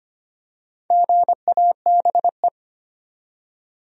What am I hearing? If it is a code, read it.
Morse code, heard as a keyed tone, about 25 wpm, 700 Hz. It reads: GABE
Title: USA Common Names - Max 4 Letters - 25wpm